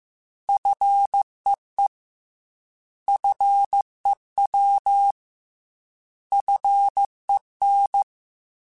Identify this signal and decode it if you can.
Morse code, heard as a keyed tone, about 15 wpm, 800 Hz. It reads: FEE FEW FEN
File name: Alphabetical / Alphabetical mp3s / Fs @13